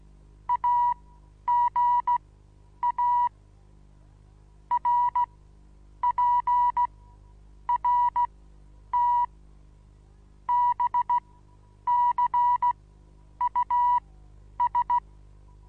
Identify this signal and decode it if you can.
Morse code, heard as a keyed tone, about 16 wpm, spaced as for 9 wpm, 970 Hz. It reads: AGA RPRT BCUS
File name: Onlinelehrgang / morsen / DJ4UF Morsen25